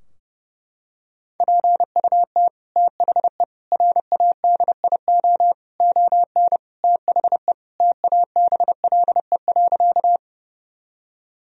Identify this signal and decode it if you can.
Morse code, heard as a keyed tone, about 30 wpm, 700 Hz. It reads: PUT THE RADIO ON THE TABLE.
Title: Sentences from Top 1000 Words - Review Part 3 - 30wpm